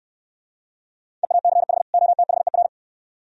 Morse code, heard as a keyed tone, about 69 wpm, 690 Hz. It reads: EACH OTHER